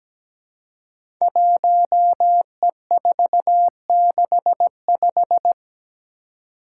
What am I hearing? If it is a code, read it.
Morse code, heard as a keyed tone, about 17 wpm, 700 Hz. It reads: 1E465
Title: Code Groups - 5 Letter:Numbers - 17wpm